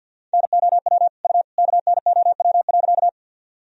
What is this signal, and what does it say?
Morse code, heard as a keyed tone, about 53 wpm, 700 Hz. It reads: NOW U KNOW.
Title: Sentences from Top 100 Words and Abbreviations 50wpm